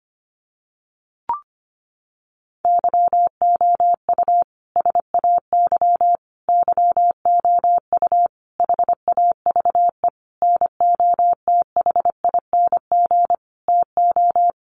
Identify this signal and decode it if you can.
Morse code, heard as a keyed tone, about 25 wpm, 700 Hz. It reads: YOU SAY YOU HAVE NOTHING TO
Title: Sentences from Top 700 Words 25wpm